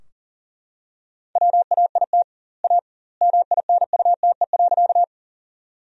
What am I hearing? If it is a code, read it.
Morse code, heard as a keyed tone, about 41 wpm, 700 Hz. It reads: WAIT A MINUTE.
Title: 2x Word-Spacing - Sentences from Top 500 Words - 40wpm